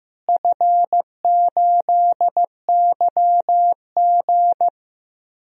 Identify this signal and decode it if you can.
Morse code, heard as a keyed tone, about 15 wpm, 700 Hz. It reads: F8YG
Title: DXCC-100 Call Signs - 4 Characters - Repeated 3x - 15wpm